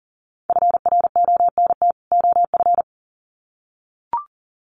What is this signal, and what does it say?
Morse code, heard as a keyed tone, about 40 wpm, 700 Hz. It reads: FRONT OF
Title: Sets of 2 Words 40wpm